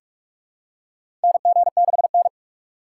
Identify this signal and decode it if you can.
Morse code, heard as a keyed tone, about 46 wpm, 700 Hz. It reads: NG6N
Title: USA Call Signs - 4 Characters - 6x Character Spacing - 45wpm